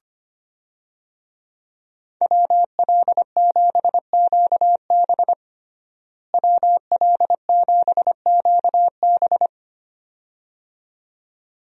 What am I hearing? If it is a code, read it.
Morse code, heard as a keyed tone, about 25 wpm, 700 Hz. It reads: WL7QB WL7QB
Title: USA Call Signs - 5 Characters - Repeated 2x - 25wpm